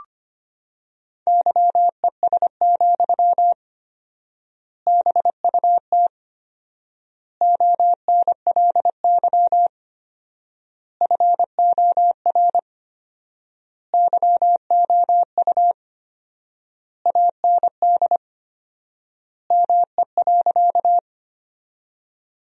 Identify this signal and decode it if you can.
Morse code, heard as a keyed tone, about 25 wpm, 700 Hz. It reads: YES, BUT ONLY FOR YOU AND ME.
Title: 4x Word-Spacing - Sentences from Top 500 Words - 25wpm